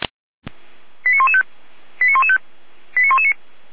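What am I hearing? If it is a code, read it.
Five-tone selective call (ZVEI), heard as quick runs of five tones, tones of about 70 ms. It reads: 89115 89115 89119
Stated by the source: SDR-received ZVEI five-tone call, code 89072.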